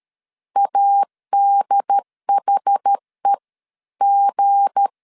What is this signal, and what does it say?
Morse code, heard as a keyed tone, about 13 wpm, spaced as for 11 wpm, 800 Hz. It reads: ADHEG
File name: LEZIONI / 8 YJ45 / 8 YJ 12wpm 2235